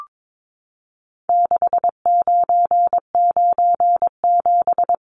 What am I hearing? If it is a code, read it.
Morse code, heard as a keyed tone, about 22 wpm, 700 Hz. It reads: 6997
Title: Numbers - 4-Digit 22wpm